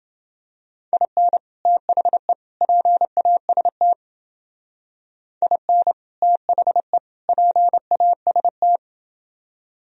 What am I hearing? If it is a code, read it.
Morse code, heard as a keyed tone, about 28 wpm, 700 Hz. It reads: IN THE PAST IN THE PAST